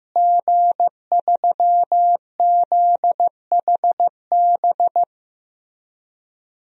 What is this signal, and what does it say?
Morse code, heard as a keyed tone, about 15 wpm, 700 Hz. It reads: G3ZHB
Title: DXCC-100 Call Signs - 5 Characters - Repeated 3x - 15wpm